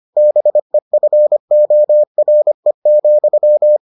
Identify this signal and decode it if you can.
Morse code, heard as a keyed tone, about 25 wpm, 600 Hz. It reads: BEFORE,